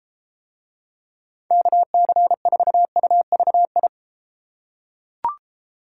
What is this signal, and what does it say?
Morse code, heard as a keyed tone, about 32 wpm, 700 Hz. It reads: KC4UVI